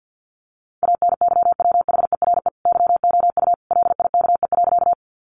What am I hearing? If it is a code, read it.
Morse code, heard as a keyed tone, about 50 wpm, 700 Hz. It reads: ANYWHERE YOU LIKE.
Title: Sentences from Top 1400 Words 50wpm